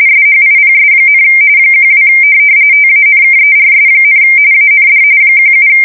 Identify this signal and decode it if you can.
RTTY, about 45.45 baud. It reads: SHIPS, 12 TANKERS, AND 17
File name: fsk170 / USS-New-Jersey